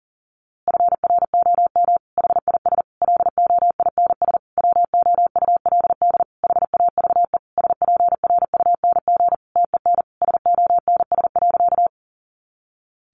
Morse code, heard as a keyed tone, about 40 wpm, 700 Hz. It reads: FROM HIS LOINS WOULD HAVE SPRUNG TEN SONS.
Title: A Sound Of Thunder 40wpm